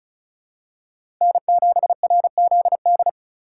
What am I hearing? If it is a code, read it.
Morse code, heard as a keyed tone, about 36 wpm, 700 Hz. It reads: N7RZD
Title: USA Call Signs - 4x Character Spacing - 35wpm